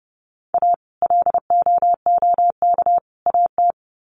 Morse code, heard as a keyed tone, about 30 wpm, 700 Hz. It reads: A LOOK AT
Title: Sets of 3 Words 30wpm